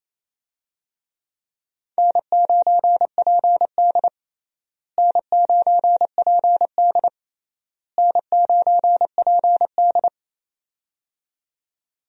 Morse code, heard as a keyed tone, about 28 wpm, 700 Hz. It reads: N9PD N9PD N9PD
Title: USA Call Signs - 4 Characters - Repeated 3x - 28wpm